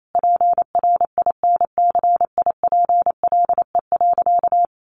Morse code, heard as a keyed tone, about 28 wpm, 700 Hz. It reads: PRINCIPLE.
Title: Sentences from Top 1400 Words 28wpm